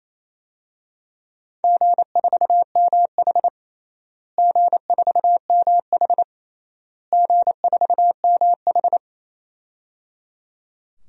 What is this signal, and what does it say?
Morse code, heard as a keyed tone, about 28 wpm, 700 Hz. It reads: G4MH G4MH G4MH
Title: DXCC-100 Call Signs - 4 Characters - Repeated 3x - 28wpm